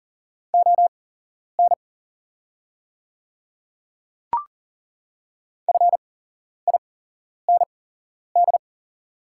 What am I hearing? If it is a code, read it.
Morse code, heard as a keyed tone, about 40 wpm, spaced as for 11 wpm, 700 Hz. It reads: ON FIND
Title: 8x Character Spacing - Top 100 Words - 40wpm